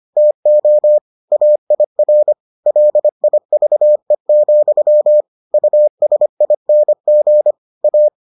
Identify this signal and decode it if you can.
Morse code, heard as a keyed tone, about 25 wpm, 600 Hz. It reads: TO AIR LIVE, USING A